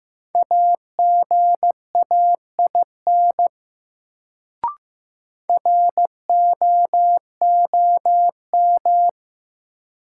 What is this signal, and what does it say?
Morse code, heard as a keyed tone, about 15 wpm, 700 Hz. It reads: AGAIN ROOM
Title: Sets of 4 Words - New Word Drill - 15wpm